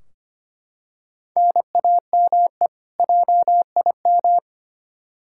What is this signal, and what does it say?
Morse code, heard as a keyed tone, about 25 wpm, 700 Hz. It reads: NAME JIM
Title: K1USN SST QSO Practice 25wpm